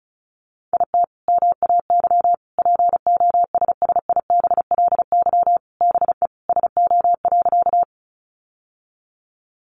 Morse code, heard as a keyed tone, about 34 wpm, 700 Hz. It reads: IT MAY POSSIBLY BE SO.
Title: Sentences from Top 1300 Words 35wpm